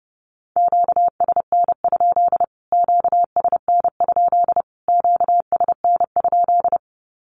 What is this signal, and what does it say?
Morse code, heard as a keyed tone, about 30 wpm, 700 Hz. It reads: QSN? QSN? QSN?
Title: Q-Codes Repeated 3x 30wpm